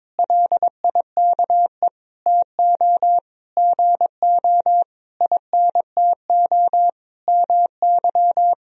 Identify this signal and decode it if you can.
Morse code, heard as a keyed tone, about 22 wpm, 700 Hz. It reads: LIKE TO GO INTO MY
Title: Sentences from Top 1000 Words - Review Part 1 - 22wpm